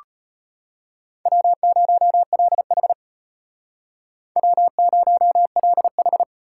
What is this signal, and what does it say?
Morse code, heard as a keyed tone, about 33 wpm, 700 Hz. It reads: W0LH W0LH